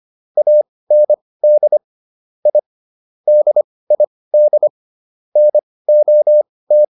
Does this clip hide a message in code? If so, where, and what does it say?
Yes: Morse code from 0.4 to 6.8 s, heard as a keyed tone, about 25 wpm, spaced as for 18 wpm, 600 Hz. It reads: AND I DID NOT